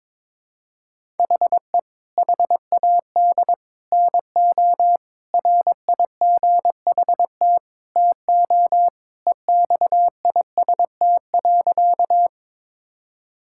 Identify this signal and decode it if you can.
Morse code, heard as a keyed tone, about 22 wpm, 700 Hz. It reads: HE HAD NO RIGHT TO EXIST.
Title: Sentences from Top 1000 Words - Review Part 5 - 22wpm